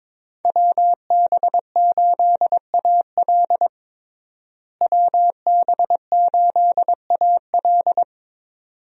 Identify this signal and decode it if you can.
Morse code, heard as a keyed tone, about 22 wpm, 700 Hz. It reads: WB8AL WB8AL
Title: USA Call Signs - 5 Characters - Repeated 2x - 22wpm